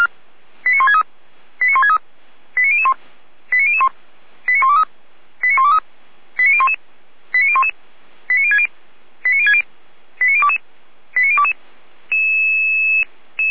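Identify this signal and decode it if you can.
Five-tone selective call (ZVEI), heard as quick runs of five tones, tones of about 70 ms. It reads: 89163 89163 89001 89001 89123 89123 89011 89011 89060 89060 89022 89022
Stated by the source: ZVEI five-tone selective call 89119, received on SDR.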